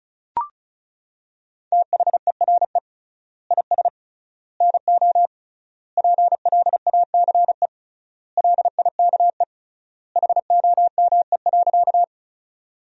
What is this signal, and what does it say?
Morse code, heard as a keyed tone, about 36 wpm, 700 Hz. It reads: THERE IS NO PLACE LIKE HOME.
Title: 3x Word-Spacing - Sentences from Top 200 Words - 35wpm